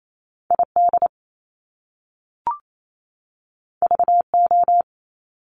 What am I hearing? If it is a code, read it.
Morse code, heard as a keyed tone, about 28 wpm, 700 Hz. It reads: ID VO
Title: Top Two Letter Combinations in Words 28wpm